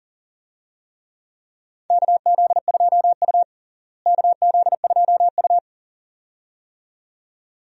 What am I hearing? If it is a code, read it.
Morse code, heard as a keyed tone, about 40 wpm, 700 Hz. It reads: KZ2U KZ2U
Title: USA Call Signs - 4 Characters - Repeated 2x - 40wpm